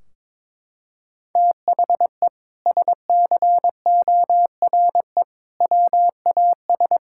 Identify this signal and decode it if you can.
Morse code, heard as a keyed tone, about 22 wpm, 700 Hz. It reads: THE SCORE WAS
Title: Sentences from Top 1600 Words 22wpm